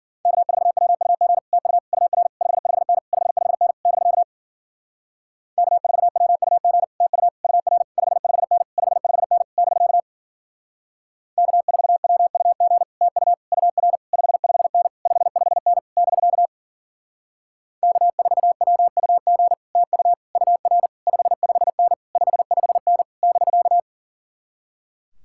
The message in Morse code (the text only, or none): K4WUG TU UR 55N 55N <BK> K4WUG TU UR 55N 55N <BK> K4WUG TU UR 55N 55N <BK> K4WUG TU UR 55N 55N <BK>